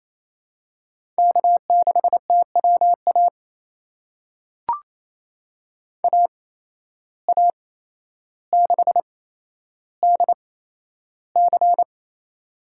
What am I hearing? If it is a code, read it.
Morse code, heard as a keyed tone, about 28 wpm, spaced as for 12 wpm, 700 Hz. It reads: K6TWA AA6DC
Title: USA Call Signs - 8x Character Spacing - 28wpm